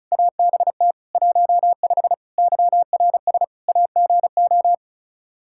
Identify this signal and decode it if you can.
Morse code, heard as a keyed tone, about 36 wpm, 700 Hz. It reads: ABT 15 YRS AGO
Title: K1USN SST QSO Practice 35wpm